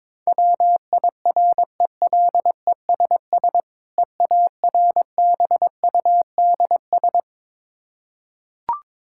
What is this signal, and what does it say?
Morse code, heard as a keyed tone, about 21 wpm, 700 Hz. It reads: WIRELESS EARBUDS
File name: Difficult Words with Hints - 22wpm